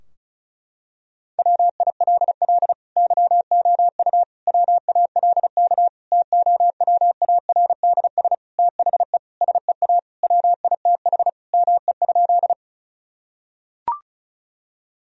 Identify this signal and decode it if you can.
Morse code, heard as a keyed tone, about 34 wpm, 700 Hz. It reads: WILL YOU WALK TOWARDS THE SEA WITH ME?
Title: Sentences from Top 1000 Words - Review Part 2 - 35wpm